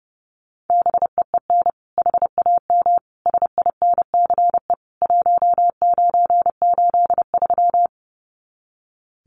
Morse code, heard as a keyed tone, about 30 wpm, 700 Hz. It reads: BEEN HAM SINCE 1983
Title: Common QSO Phrases 30wpm